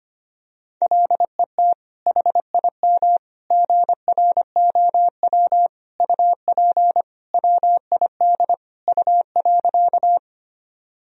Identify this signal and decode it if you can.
Morse code, heard as a keyed tone, about 25 wpm, 700 Hz. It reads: LET HIM GROW UP WID U.
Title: Sentences from Top 300 Words and Abbreviations 25wpm